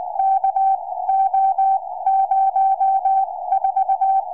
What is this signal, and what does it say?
Morse code, heard as a keyed tone, about 20 wpm, spaced as for 16 wpm, 770 Hz. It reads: KO04
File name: tropo / ua2fl 2m